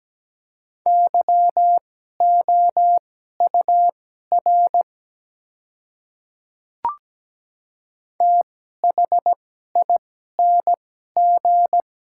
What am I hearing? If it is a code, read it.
Morse code, heard as a keyed tone, about 17 wpm, spaced as for 12 wpm, 700 Hz. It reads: YOUR THING
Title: 2x Character Spacing - Top 100 Words - 17wpm